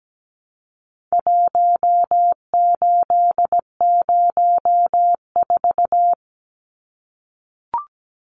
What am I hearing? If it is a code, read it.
Morse code, heard as a keyed tone, about 17 wpm, 700 Hz. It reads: 1804
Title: Numbers - Contest Style - 4-Digit 17wpm